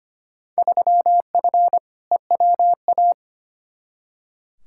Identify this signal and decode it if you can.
Morse code, heard as a keyed tone, about 25 wpm, 700 Hz. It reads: 3F EWA
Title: Field Day Exchanges Encore - 25wpm